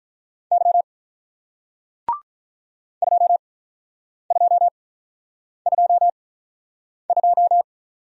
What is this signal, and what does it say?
Morse code, heard as a keyed tone, about 48 wpm, spaced as for 9 wpm, 700 Hz. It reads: Y 2222